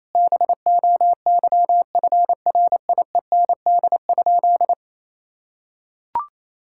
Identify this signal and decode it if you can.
Morse code, heard as a keyed tone, about 28 wpm, 700 Hz. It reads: BOYFRIEND?
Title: Sentences from Top 1000 Words - Review Part 5 - 28wpm